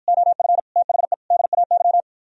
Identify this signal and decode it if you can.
Morse code, heard as a keyed tone, about 52 wpm, 700 Hz. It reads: OF THE DAY